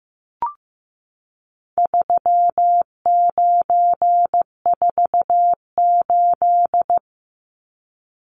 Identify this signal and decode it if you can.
Morse code, heard as a keyed tone, about 15 wpm, 700 Hz. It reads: 3948
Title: Numbers - 4-Digit 15wpm